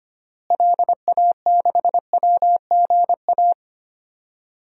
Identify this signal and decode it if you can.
Morse code, heard as a keyed tone, about 25 wpm, 700 Hz. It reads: LA6WGA
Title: DXCC-100 Call Signs - 6 Characters - Repeated 2x - 25wpm